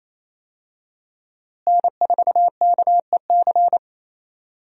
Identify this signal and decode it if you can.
Morse code, heard as a keyed tone, about 28 wpm, 700 Hz. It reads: N4KEC
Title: USA Call Signs - 5 Characters - Repeated 3x - 28wpm